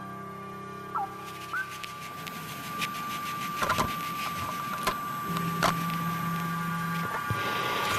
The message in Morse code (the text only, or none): none